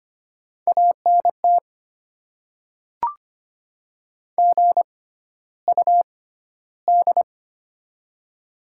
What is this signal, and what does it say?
Morse code, heard as a keyed tone, about 25 wpm, spaced as for 13 wpm, 700 Hz. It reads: ANT GUD
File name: Top 20 QSO Elements - 6x Character Spacing - 25wpm